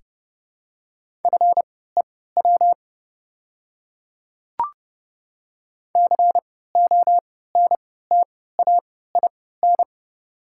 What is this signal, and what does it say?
Morse code, heard as a keyed tone, about 30 wpm, spaced as for 17 wpm, 690 Hz. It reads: FEW CONTAIN
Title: 3x Character Spacing - Top 500 Words - 30wpm